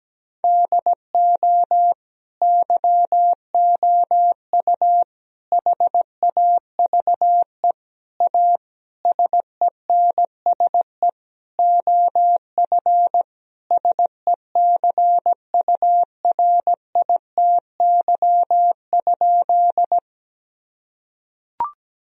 Morse code, heard as a keyed tone, about 17 wpm, 700 Hz. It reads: DO YOU HAVE A SENSE OF SECURITY?
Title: Sentences from Top 1000 Words - Review Part 4 - 17wpm